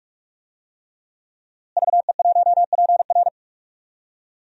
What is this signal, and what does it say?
Morse code, heard as a keyed tone, about 43 wpm, 700 Hz. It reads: VE1PR
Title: DXCC-100 Call Signs - 5 Characters - Repeated 3x - 45wpm